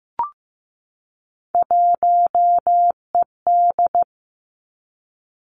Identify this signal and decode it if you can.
Morse code, heard as a keyed tone, about 15 wpm, 700 Hz. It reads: 1ED
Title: Three Challenging Letter-Number Combinations 15wpm